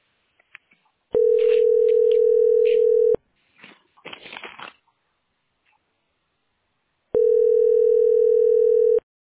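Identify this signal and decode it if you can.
Ringback tone (North America), 440 + 480 Hz.
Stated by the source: Phone call recorded on the tesspb.mdpbx PBX.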